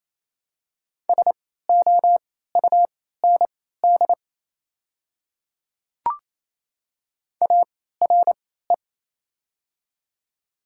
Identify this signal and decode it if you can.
Morse code, heard as a keyed tone, about 28 wpm, spaced as for 16 wpm, 700 Hz. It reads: SOUND ARE
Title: 3x Character Spacing - Top 100 Words - 28wpm